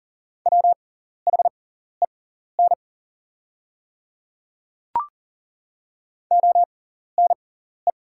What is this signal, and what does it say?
Morse code, heard as a keyed tone, about 38 wpm, spaced as for 14 wpm, 700 Hz. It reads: WHEN ONE
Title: 6x Character Spacing - Top 200 Words - 40wpm